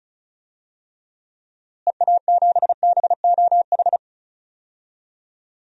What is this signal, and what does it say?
Morse code, heard as a keyed tone, about 36 wpm, 700 Hz. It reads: EA7BOH
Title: DXCC-100 Call Signs - 6 Characters - Repeated 3x - 35wpm